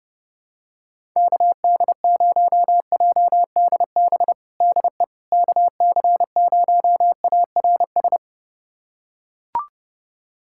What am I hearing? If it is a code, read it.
Morse code, heard as a keyed tone, about 30 wpm, 700 Hz. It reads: KD0JDB DE KC0ARS
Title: USA Callsign DE Callsign - 6 Characters - 30wpm